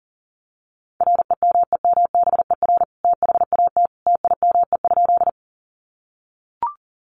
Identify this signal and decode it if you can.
Morse code, heard as a keyed tone, about 40 wpm, 700 Hz. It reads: REMEMBER THAT TIME?
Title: Sentences from Top 400 Words 40wpm